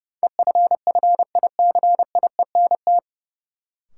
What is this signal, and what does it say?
Morse code, heard as a keyed tone, about 30 wpm, 700 Hz. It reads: EFFICIENT
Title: Sentences from Top 1300 Words - New Word Drill - 30wpm